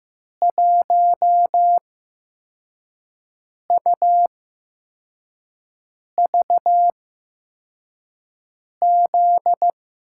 Morse code, heard as a keyed tone, about 15 wpm, spaced as for 4 wpm, 700 Hz. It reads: 1UVZ
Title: USA Call Signs - 5 Characters - 8x Character Spacing - 15wpm